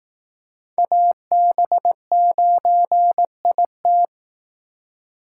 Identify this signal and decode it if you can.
Morse code, heard as a keyed tone, about 18 wpm, 700 Hz. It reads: AB9IT
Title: Speed Racing - USA 2x2 Call Signs 15wpm